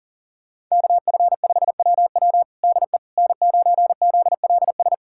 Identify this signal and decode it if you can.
Morse code, heard as a keyed tone, about 40 wpm, 700 Hz. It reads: KF5WW DE N9ZLS
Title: USA Callsign DE Callsign - 5 Characters - 40wpm